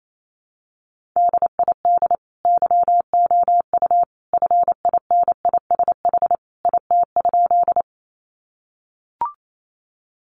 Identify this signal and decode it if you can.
Morse code, heard as a keyed tone, about 28 wpm, 700 Hz. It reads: DID YOU FINISH IT?
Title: Sentences from Top 1200 Words 28wpm